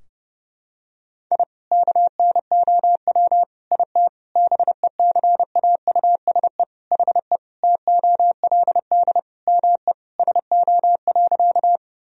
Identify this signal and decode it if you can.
Morse code, heard as a keyed tone, about 29 wpm, 700 Hz. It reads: I KNOW IT BECAUSE HE TOLD ME SO.
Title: Sentences from Top 1000 Words - Review Part 3 - 30wpm